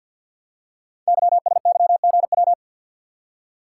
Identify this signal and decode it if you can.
Morse code, heard as a keyed tone, about 50 wpm, 700 Hz. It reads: YSYGW
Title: Code Groups - 5 Letters - 50wpm